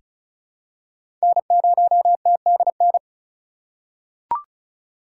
Morse code, heard as a keyed tone, about 36 wpm, 700 Hz. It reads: N0TDN
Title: USA Call Signs - 8x Character Spacing - 35wpm